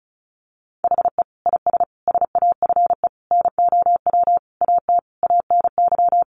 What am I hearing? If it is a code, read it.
Morse code, heard as a keyed tone, about 34 wpm, 700 Hz. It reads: HE IS SAFE NOW AT ANY
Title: Sentences from Top 900 Words 35wpm